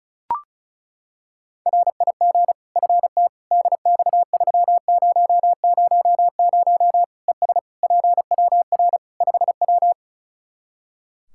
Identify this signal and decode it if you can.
Morse code, heard as a keyed tone, about 36 wpm, 700 Hz. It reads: RIG FT DX3000 ES PWR 5W
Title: K1USN SST QSO Practice 35wpm